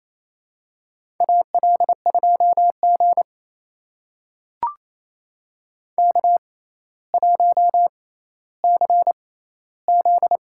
Morse code, heard as a keyed tone, about 27 wpm, spaced as for 14 wpm, 700 Hz. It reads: AL2G K1CZ
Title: USA Call Signs - 4 Characters - 6x Character Spacing - 28wpm